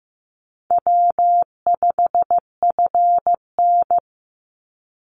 Morse code, heard as a keyed tone, about 15 wpm, 700 Hz. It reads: W5FN
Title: Easy Call Signs Repeated 3x - 15wpm